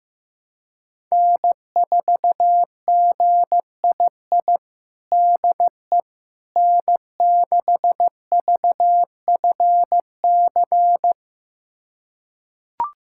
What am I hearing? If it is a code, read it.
Morse code, heard as a keyed tone, about 15 wpm, 700 Hz. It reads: N4GII DE N6VFC
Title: USA Callsign DE Callsign - 5 Characters - 15wpm